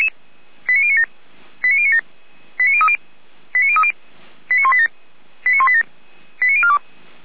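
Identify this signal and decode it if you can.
Five-tone selective call (ZVEI), heard as quick runs of five tones, tones of about 70 ms. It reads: 89097 89097 89030 89030 89187 89187 89042
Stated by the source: ZVEI five-tone selective call 89229 received by SDR.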